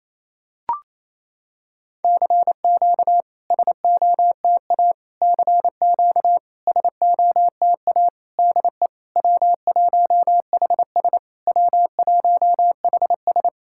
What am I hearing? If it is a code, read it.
Morse code, heard as a keyed tone, about 28 wpm, 700 Hz. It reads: CQ SOTA CQ SOTA DE W1HS W1HS
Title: SOTA QSO Practice 28wpm